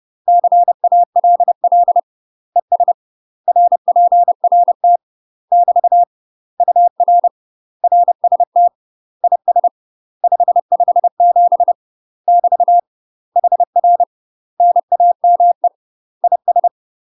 Morse code, heard as a keyed tone, about 30 wpm, 700 Hz. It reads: CALL ES RPRT <BT> UR RST IS 557 <BT> HR NAME IS
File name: UK MASTER QSOs TRAINING/30WPM